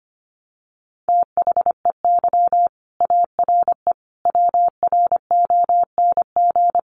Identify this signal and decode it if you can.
Morse code, heard as a keyed tone, about 25 wpm, 700 Hz. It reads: THEY ARE WRONG
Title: Sentences from Top 900 Words 25wpm